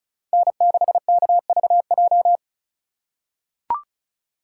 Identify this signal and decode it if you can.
Morse code, heard as a keyed tone, about 34 wpm, 700 Hz. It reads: N6KVJ
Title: USA Call Signs - 5 Characters - 4x Character Spacing - 35wpm